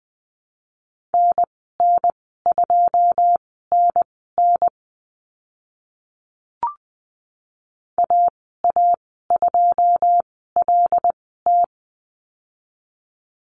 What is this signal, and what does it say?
Morse code, heard as a keyed tone, about 20 wpm, spaced as for 14 wpm, 700 Hz. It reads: NN2NN AA2LT
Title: 082 Lesson 14g - Number 2 - Call Signs 20wpm